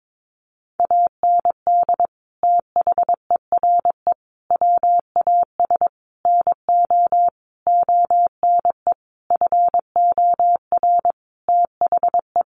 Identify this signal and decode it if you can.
Morse code, heard as a keyed tone, about 22 wpm, 700 Hz. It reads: AND THERE WAS NO ONE FOR THE
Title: Sentences from Top 1500 Words 22wpm